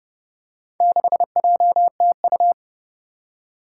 Morse code, heard as a keyed tone, about 30 wpm, 700 Hz. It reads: 6JTU
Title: Four Challenging Letter-Number Combinations 30wpm